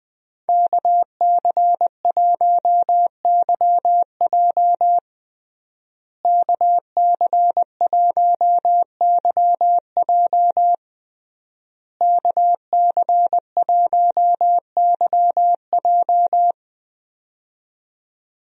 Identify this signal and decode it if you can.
Morse code, heard as a keyed tone, about 20 wpm, 700 Hz. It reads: KC1YJ KC1YJ KC1YJ
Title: USA Call Signs - 5 Characters - Repeated 3x - 20wpm